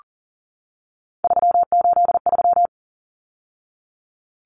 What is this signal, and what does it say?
Morse code, heard as a keyed tone, about 40 wpm, 700 Hz. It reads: 383
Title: Numbers - 3-Digit 40wpm